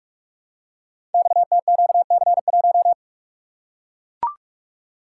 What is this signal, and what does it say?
Morse code, heard as a keyed tone, about 44 wpm, 700 Hz. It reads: XTQC1
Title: Code Groups - 5 Letter:Numbers - 45wpm